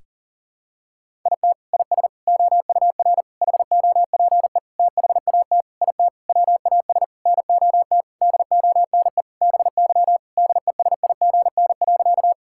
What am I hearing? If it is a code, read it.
Morse code, heard as a keyed tone, about 40 wpm, 700 Hz. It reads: IT IS OUR HOPE THAT IT WAS NOT DONE BY DESIGN.
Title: Sentences from Top 1000 Words - Review Part 5 - 40wpm